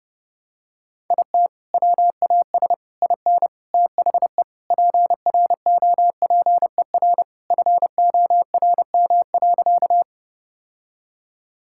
Morse code, heard as a keyed tone, about 32 wpm, 700 Hz. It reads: IT WAS IN THE PROPER FORM.
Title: Sentences from Top 1000 Words - Review Part 5 - 30wpm